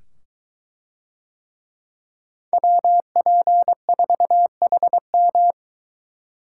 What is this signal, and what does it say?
Morse code, heard as a keyed tone, about 23 wpm, 700 Hz. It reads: WP4HM